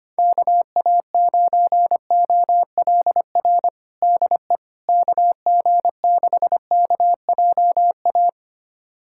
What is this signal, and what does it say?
Morse code, heard as a keyed tone, about 25 wpm, 700 Hz. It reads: KA9OLR DE KG6KJA
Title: USA Callsign DE Callsign - 6 Characters - 25wpm